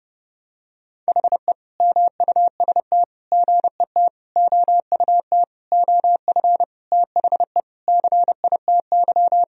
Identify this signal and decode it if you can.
Morse code, heard as a keyed tone, about 32 wpm, 700 Hz. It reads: HE MUST GET OUT OF THE CITY
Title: Sentences from Top 1000 Words - Review Part 1 - 30wpm